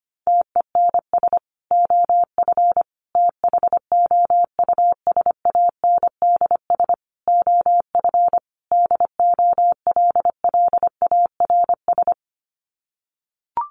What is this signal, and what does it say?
Morse code, heard as a keyed tone, about 25 wpm, 700 Hz. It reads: TENS OF THOUSANDS OF DOLLARS
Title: Sets of 5 Words Encore2 25wpm